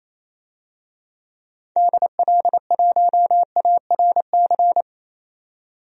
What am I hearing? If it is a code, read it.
Morse code, heard as a keyed tone, about 28 wpm, 700 Hz. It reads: DL1ARC
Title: DXCC-100 Call Signs - 6 Characters - 28wpm